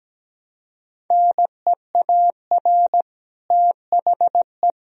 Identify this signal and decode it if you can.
Morse code, heard as a keyed tone, about 17 wpm, 700 Hz. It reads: NEAR THE
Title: Sentences from Top 1000 Words - Review Part 4 - 17wpm